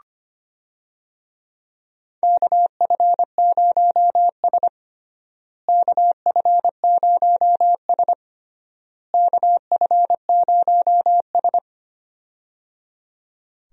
Morse code, heard as a keyed tone, about 25 wpm, 700 Hz. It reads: KF0S KF0S KF0S
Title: USA Call Signs - 4 Characters - Repeated 3x - 25wpm